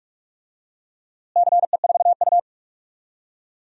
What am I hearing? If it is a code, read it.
Morse code, heard as a keyed tone, about 47 wpm, 700 Hz. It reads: CE4U
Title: DXCC-100 Call Signs - 4 Characters - Repeated 2x - 45wpm